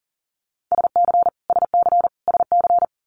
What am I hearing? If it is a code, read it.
Morse code, heard as a keyed tone, about 38 wpm, 700 Hz. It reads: SC SC SC
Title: Multi-letter Phonemes Repeated 3x 40wpm